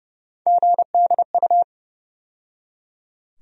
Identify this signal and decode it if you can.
Morse code, heard as a keyed tone, about 30 wpm, 700 Hz. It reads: GDU
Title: USA Call Sign Postfixes - 3 Letters 30wpm